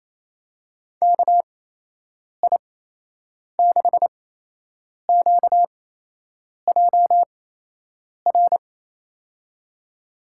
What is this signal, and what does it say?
Morse code, heard as a keyed tone, about 28 wpm, spaced as for 8 wpm, 700 Hz. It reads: KI6QJR
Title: USA Call Signs - 8x Character Spacing - 28wpm